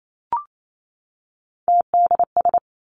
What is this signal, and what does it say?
Morse code, heard as a keyed tone, about 28 wpm, 700 Hz. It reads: TDS
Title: Three Challenging Letter-Number Combinations 28wpm